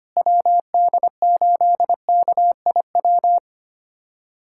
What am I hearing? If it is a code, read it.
Morse code, heard as a keyed tone, about 25 wpm, 700 Hz. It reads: WD8KIW
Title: USA Callsign DE Callsign - 6 Characters - 25wpm